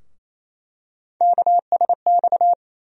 Morse code, heard as a keyed tone, about 28 wpm, 700 Hz. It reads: KSX
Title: USA Call Sign Postfixes - 3 Letters 28wpm